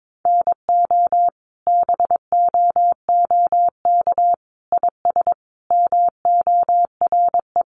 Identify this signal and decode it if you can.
Morse code, heard as a keyed tone, about 22 wpm, 700 Hz. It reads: NO BOOK IS MORE
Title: Sentences from Top 1200 Words 22wpm